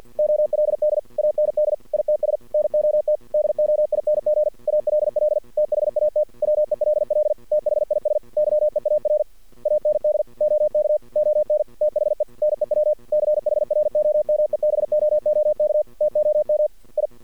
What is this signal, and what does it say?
Morse code, heard as a keyed tone, about 53 wpm, 610 Hz. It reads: OLD AND ITS NOT DOING ALL THAT WELL THIS YEAR AND OK ON THE NEW CHROMEBOOK TOM I
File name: archived / CRSnet